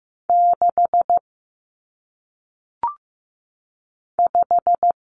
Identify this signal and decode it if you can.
Morse code, heard as a keyed tone, about 15 wpm, 700 Hz. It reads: 6 5
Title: Numbers - 1-Digit 15wpm